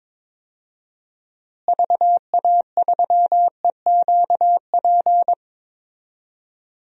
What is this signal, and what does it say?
Morse code, heard as a keyed tone, about 22 wpm, 700 Hz. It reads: VA3EQP
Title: DXCC-100 Call Signs - 6 Characters - Repeated 3x - 22wpm